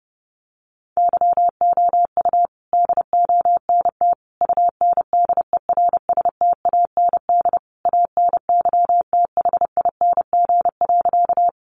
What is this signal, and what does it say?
Morse code, heard as a keyed tone, about 30 wpm, 700 Hz. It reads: YOU DONT UNDERSTAND ANYTHING.
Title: Sentences from Top 1000 Words 30wpm